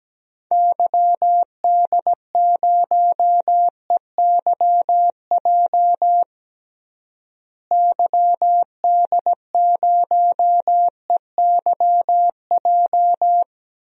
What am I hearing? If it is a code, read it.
Morse code, heard as a keyed tone, about 17 wpm, 700 Hz. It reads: YD0EYJ YD0EYJ
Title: DXCC-100 Call Signs - 6 Characters - Repeated 2x - 17wpm